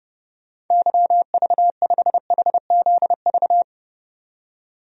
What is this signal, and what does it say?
Morse code, heard as a keyed tone, about 30 wpm, 700 Hz. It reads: YV5HZV
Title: DXCC-100 Call Signs - 6 Characters - 30wpm